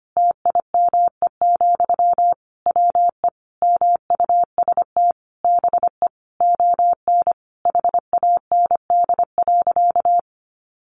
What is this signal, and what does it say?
Morse code, heard as a keyed tone, about 25 wpm, 700 Hz. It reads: TIME, WE MUST BE ON HAND.
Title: Sentences from Top 1500 Words 25wpm